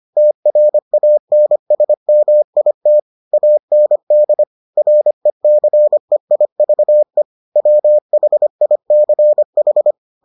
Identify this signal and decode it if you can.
Morse code, heard as a keyed tone, about 25 wpm, 600 Hz. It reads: TRANSMIT AND RECEIVE WHICH